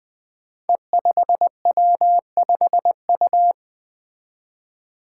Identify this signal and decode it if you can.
Morse code, heard as a keyed tone, about 20 wpm, 700 Hz. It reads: E5W5U
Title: Five Challenging Letter-Number Combinations 20wpm